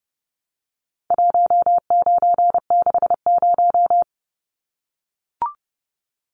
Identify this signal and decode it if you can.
Morse code, heard as a keyed tone, about 30 wpm, 700 Hz. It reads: 1960
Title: Numbers - 4-Digit 30wpm